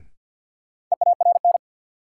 Morse code, heard as a keyed tone, about 53 wpm, 700 Hz. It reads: EARN
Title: Sentences from Top 1500 Words - New Word Drill Alt - 50wpm